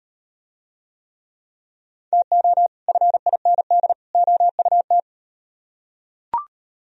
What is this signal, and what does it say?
Morse code, heard as a keyed tone, about 38 wpm, 700 Hz. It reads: TO FIND OUT